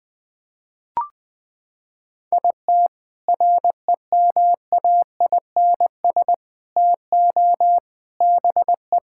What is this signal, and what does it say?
Morse code, heard as a keyed tone, about 20 wpm, 700 Hz. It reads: IT REMAINS TO BE
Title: Sentences with 7-Word Phrases - 20wpm